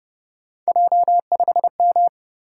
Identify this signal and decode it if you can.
Morse code, heard as a keyed tone, about 31 wpm, 700 Hz. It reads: J5M
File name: Speed-Racing - Three Letter-Number Combinations - 25wpm